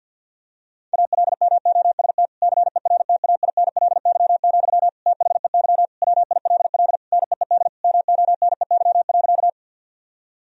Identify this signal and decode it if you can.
Morse code, heard as a keyed tone, about 50 wpm, 700 Hz. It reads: ALMOST CERTAINLY, THEY WILL NEED MONEY.
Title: Sentences from Top 1000 Words - Review Part 4 - 50wpm